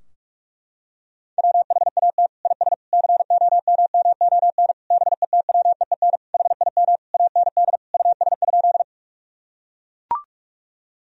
Morse code, heard as a keyed tone, about 42 wpm, 700 Hz. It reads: WHAT IS COMMON BETWEEN HIM AND US?
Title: Sentences from Top 1000 Words - Review Part 3 - 45wpm